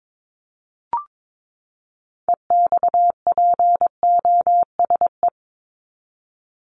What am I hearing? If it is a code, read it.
Morse code, heard as a keyed tone, about 22 wpm, 700 Hz. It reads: EXPOSE
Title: Top 1501-1750 Words 22wpm